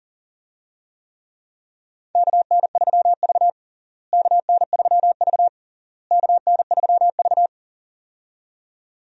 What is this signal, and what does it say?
Morse code, heard as a keyed tone, about 40 wpm, 700 Hz. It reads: KN3V KN3V KN3V
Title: USA Call Signs - 4 Characters - Repeated 3x - 40wpm